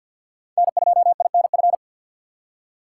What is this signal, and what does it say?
Morse code, heard as a keyed tone, about 51 wpm, 700 Hz. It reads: N2INF
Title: USA Call Signs - 5 Characters - 4x Character Spacing - 50wpm